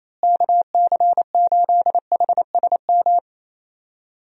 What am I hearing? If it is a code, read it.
Morse code, heard as a keyed tone, about 27 wpm, 700 Hz. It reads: KC8HSM
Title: USA Call Signs - 6 Characters - Repeated 2x - 28wpm